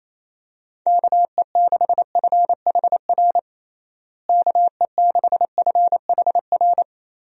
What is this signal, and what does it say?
Morse code, heard as a keyed tone, about 28 wpm, 700 Hz. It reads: KE6FHR KE6FHR
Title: USA Call Signs - 6 Characters - Repeated 2x - 28wpm